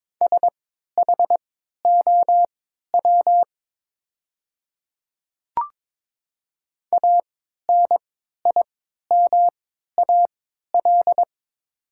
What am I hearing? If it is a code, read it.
Morse code, heard as a keyed tone, about 22 wpm, spaced as for 13 wpm, 690 Hz. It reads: SHOW ANIMAL
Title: 3x Character Spacing - Top 200 Words - 22wpm